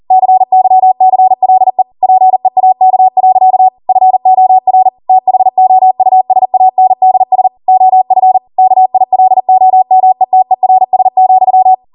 Morse code, heard as a keyed tone, about 40 wpm, 750 Hz. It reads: CYCLE PEAK. FOR THOUSANDS OF KILOMETERS,